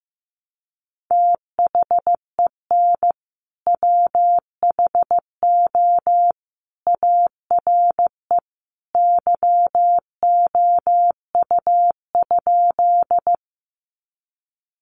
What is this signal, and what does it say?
Morse code, heard as a keyed tone, about 15 wpm, 700 Hz. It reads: THEN WHO ARE YOU?
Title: Sentences from Top 100 Words 15wpm